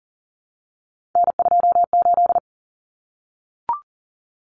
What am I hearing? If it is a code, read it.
Morse code, heard as a keyed tone, about 40 wpm, 700 Hz. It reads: N28
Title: Common Cut Numbers - 3-Digit 40wpm